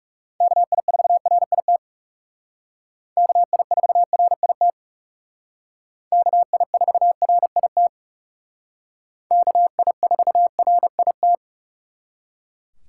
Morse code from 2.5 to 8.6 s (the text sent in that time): KI4RIT KI4RIT